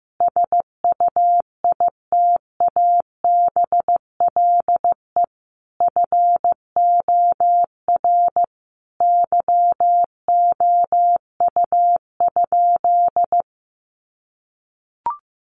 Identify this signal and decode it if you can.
Morse code, heard as a keyed tone, about 15 wpm, 700 Hz. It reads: SUITABLE FOR YOU?
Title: Sentences from Top 1400 Words 15wpm